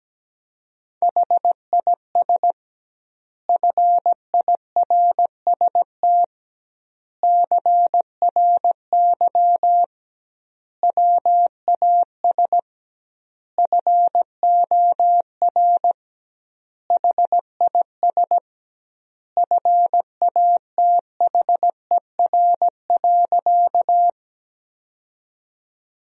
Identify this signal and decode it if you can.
Morse code, heard as a keyed tone, about 17 wpm, 700 Hz. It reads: HIS FIRST CRY WAS FOR HIS FATHER.
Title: 2x Word-Spacing - Sentences from Top 500 Words - 17wpm